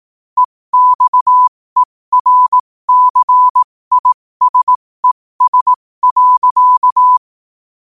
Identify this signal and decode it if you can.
Morse code, heard as a keyed tone, about 18 wpm, spaced as for 15 wpm, 1000 Hz. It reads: EXERCISES.
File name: qotdfiles / 15wpm